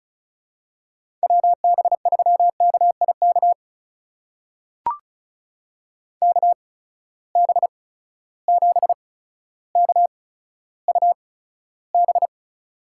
Morse code, heard as a keyed tone, about 36 wpm, spaced as for 15 wpm, 700 Hz. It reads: WB3KIK KB7KUB
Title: USA Call Signs - 6 Characters - 8x Character Spacing - 35wpm